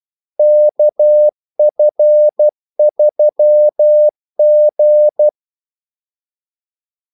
Morse code, heard as a keyed tone, about 12 wpm, 600 Hz. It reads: KF3G
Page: 2 ekstra ordmellemrum mellem kaldesignalerne